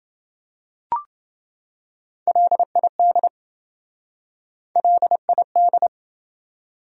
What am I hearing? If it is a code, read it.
Morse code, heard as a keyed tone, about 27 wpm, 700 Hz. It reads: LID LID